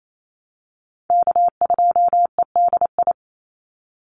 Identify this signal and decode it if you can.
Morse code, heard as a keyed tone, about 28 wpm, 700 Hz. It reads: K2EDI
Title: USA Call Signs 28wpm